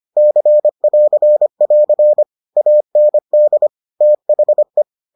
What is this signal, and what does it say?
Morse code, heard as a keyed tone, about 25 wpm, 600 Hz. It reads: C<AR><AR> AND THE